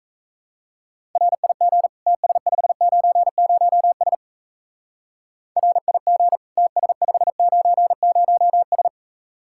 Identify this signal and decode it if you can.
Morse code, heard as a keyed tone, about 34 wpm, 700 Hz. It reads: RIG TS590S RIG TS590S
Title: Speed-Racing Common QSO Phrases - 28wpm